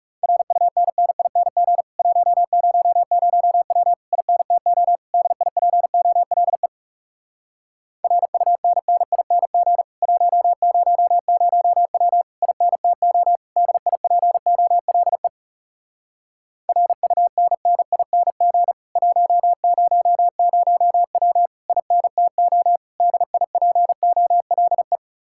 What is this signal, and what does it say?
Morse code, heard as a keyed tone, about 37 wpm, 700 Hz. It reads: RUNNING 100W INTO DIPOLE RUNNING 100W INTO DIPOLE RUNNING 100W INTO DIPOLE
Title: Speed-Racing Common QSO Phrases - 30wpm